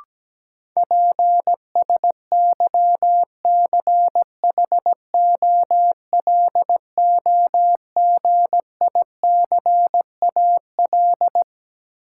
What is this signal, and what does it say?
Morse code, heard as a keyed tone, about 17 wpm, 700 Hz. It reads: PSYCHOLOGICAL
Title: Sentences from Top 1400 Words - New Word Drill Alt - 17wpm